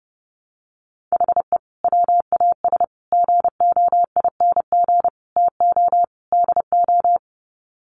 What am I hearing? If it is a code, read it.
Morse code, heard as a keyed tone, about 30 wpm, 700 Hz. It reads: HE WAS GOING TO DO
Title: Sets of 5 Words Encore2 30wpm